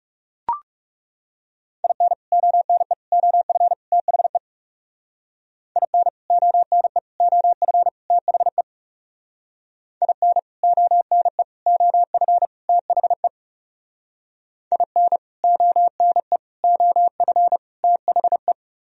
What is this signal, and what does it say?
Morse code, heard as a keyed tone, about 38 wpm, 700 Hz. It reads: IN ONE OF THE IN ONE OF THE IN ONE OF THE IN ONE OF THE
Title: Speed-Racing - Sets of 4 Words 30wpm